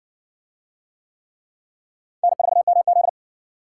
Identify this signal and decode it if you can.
Morse code, heard as a keyed tone, about 59 wpm, 700 Hz. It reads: N4MZ